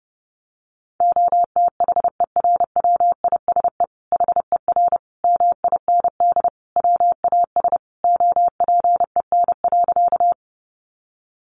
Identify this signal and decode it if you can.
Morse code, heard as a keyed tone, about 30 wpm, 700 Hz. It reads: OTHERWISE HER MIND WAS OPEN.
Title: Sentences from Top 1200 Words 30wpm